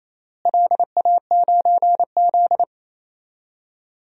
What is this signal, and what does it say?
Morse code, heard as a keyed tone, about 28 wpm, 700 Hz. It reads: LA9Z
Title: DXCC-100 Call Signs - 4 Characters - Repeated 2x - 28wpm